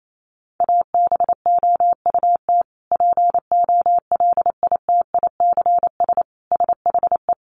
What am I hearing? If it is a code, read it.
Morse code, heard as a keyed tone, about 28 wpm, 700 Hz. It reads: ABOUT POLITICS SHE
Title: Sentences from Top 1000 Words 28wpm